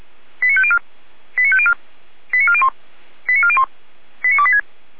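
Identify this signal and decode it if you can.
Five-tone selective call (ZVEI), heard as quick runs of five tones, tones of about 70 ms. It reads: 89504 89504 89401 89401 89286
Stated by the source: ZVEI five-tone selective call received via SDR.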